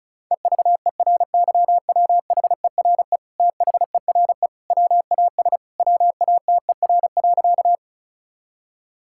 Morse code, heard as a keyed tone, about 36 wpm, 700 Hz. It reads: EVERYWHERE THERE WAS WATER.
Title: Sentences from Top 1600 Words 35wpm